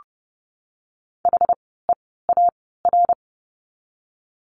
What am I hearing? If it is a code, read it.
Morse code, heard as a keyed tone, about 30 wpm, spaced as for 17 wpm, 700 Hz. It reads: HEAR
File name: course / 30:10wpm / 230 Lesson 37f - Letter x - Words 30wpm